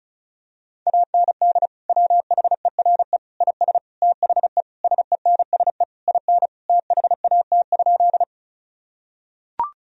Morse code, heard as a keyed tone, about 36 wpm, 700 Hz. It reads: AND WHERE IS THE SENSE IN THAT?
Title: Sentences from Top 1000 Words - Review Part 4 - 35wpm